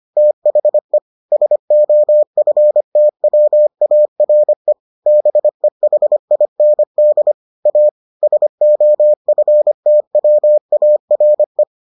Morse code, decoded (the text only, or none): THE SOFTWARE BEHIND A SOFTWARE